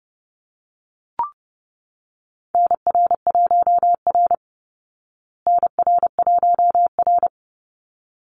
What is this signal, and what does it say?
Morse code, heard as a keyed tone, about 30 wpm, 700 Hz. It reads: NR1R NR1R
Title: Call Signs Repeated 3x - 30wpm